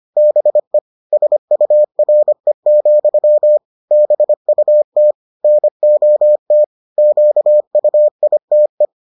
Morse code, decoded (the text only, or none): BE SURE, BUT NOT QUITE